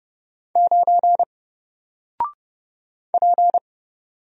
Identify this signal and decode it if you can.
Morse code, heard as a keyed tone, about 30 wpm, 700 Hz. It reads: 9 P
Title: Single Character (Letter:Number:Punctuation:Prosign) 30wpm